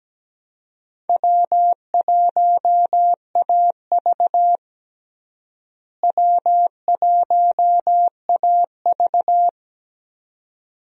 Morse code, heard as a keyed tone, about 17 wpm, 700 Hz. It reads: W1AV W1AV
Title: USA Call Signs - 4 Characters - Repeated 2x - 17wpm